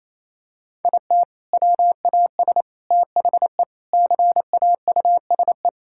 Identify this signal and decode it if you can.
Morse code, heard as a keyed tone, about 28 wpm, 700 Hz. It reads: IT WAS THE CAUSE
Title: Sentences from Top 1600 Words 28wpm